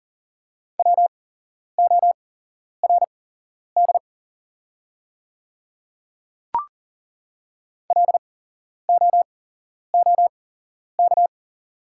Morse code, heard as a keyed tone, about 40 wpm, spaced as for 11 wpm, 700 Hz. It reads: WORD LOOK
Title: 8x Character Spacing - Top 100 Words - 40wpm